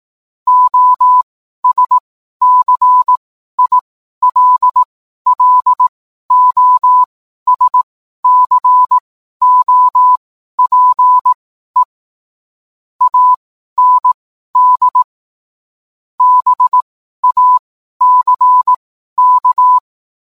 Morse code, heard as a keyed tone, about 18 wpm, spaced as for 12 wpm, 1000 Hz. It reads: OSCILLOSCOPE AND BACK